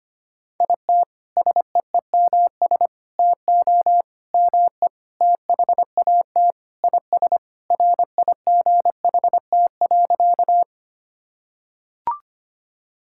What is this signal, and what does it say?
Morse code, heard as a keyed tone, about 25 wpm, 700 Hz. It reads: IT SEEMS TO ME THAT IS RIGHT.
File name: practice / 25wpm / Sentences with 4-Word Phrases - 25wpm